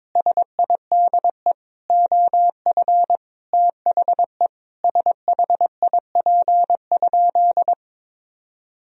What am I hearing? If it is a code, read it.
Morse code, heard as a keyed tone, about 21 wpm, 700 Hz. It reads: SIDE OF THE SHIP?
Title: Sentences from Top 1000 Words - Review Part 2 - 22wpm